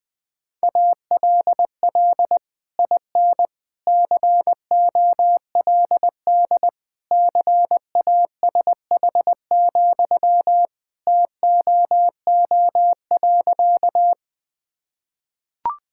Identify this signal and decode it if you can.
Morse code, heard as a keyed tone, about 20 wpm, 700 Hz. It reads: ALL IN COLD CASH, TOO.
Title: Sentences from Top 1000 Words - Review Part 5 - 20wpm